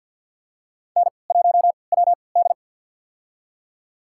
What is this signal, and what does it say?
Morse code, heard as a keyed tone, about 47 wpm, spaced as for 27 wpm, 700 Hz. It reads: N1WD
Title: USA Call Signs - 4 Characters - 3x Character Spacing - 50wpm